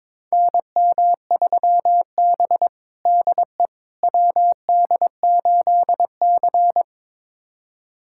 Morse code, heard as a keyed tone, about 22 wpm, 700 Hz. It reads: NM3B DE WD8C
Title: USA Callsign DE Callsign - 4 Characters - 22wpm